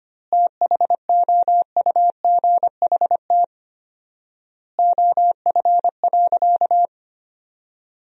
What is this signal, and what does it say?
Morse code, heard as a keyed tone, about 25 wpm, 700 Hz. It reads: THOUGHT OF.
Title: 4x Word-Spacing - Sentences from Top 300 Words - 25wpm